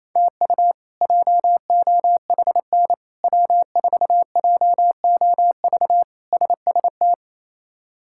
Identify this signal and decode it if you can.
Morse code, heard as a keyed tone, about 28 wpm, 700 Hz. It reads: TU JOHN W4JOV SST
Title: K1USN SST QSO Practice - 28wpm